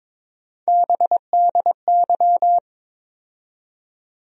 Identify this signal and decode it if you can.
Morse code, heard as a keyed tone, about 21 wpm, 700 Hz. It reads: BDY
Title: USA Call Sign Postfixes - 3 Letters 22wpm